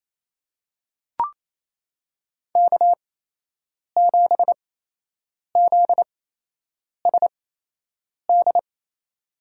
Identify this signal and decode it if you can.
Morse code, heard as a keyed tone, about 28 wpm, spaced as for 8 wpm, 700 Hz. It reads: K7ZSD
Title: USA Call Signs - 8x Character Spacing - 28wpm